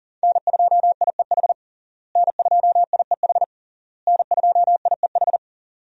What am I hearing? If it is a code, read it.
Morse code, heard as a keyed tone, about 40 wpm, 700 Hz. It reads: N2IEH N2IEH N2IEH
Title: USA Call Signs - 5 Characters - Repeated 3x - 40wpm